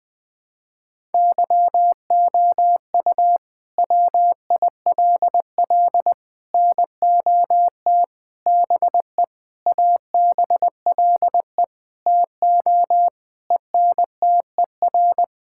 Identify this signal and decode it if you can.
Morse code, heard as a keyed tone, about 20 wpm, 700 Hz. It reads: YOU WILL NOT BE ABLE TO ENTER
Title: Sentences with 6-Word Phrases - 20wpm